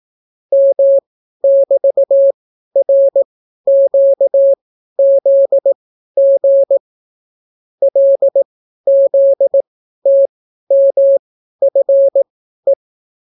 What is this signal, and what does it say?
Morse code, heard as a keyed tone, about 18 wpm, spaced as for 12 wpm, 550 Hz. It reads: M<BT>RQZG LZTMFE